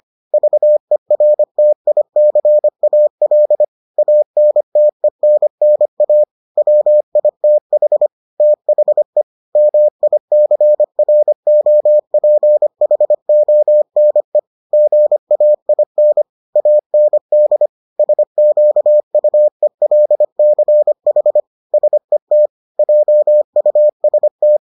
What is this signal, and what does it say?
Morse code, heard as a keyed tone, about 25 wpm, 600 Hz. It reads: VERTICAL ANTENNA WITH THE MICROPHONE GAIN AND SQUELCH SET JUST